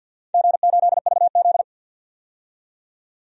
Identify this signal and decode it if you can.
Morse code, heard as a keyed tone, about 51 wpm, 700 Hz. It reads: G8VZ
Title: DXCC-100 Call Signs - 4 Characters - 50wpm